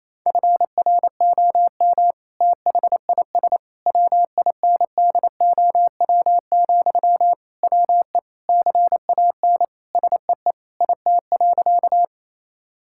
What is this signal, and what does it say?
Morse code, heard as a keyed tone, about 28 wpm, 700 Hz. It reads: FROM THIS WINDOW, WE CAN SEE IT.
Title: Sentences from Top 1600 Words 28wpm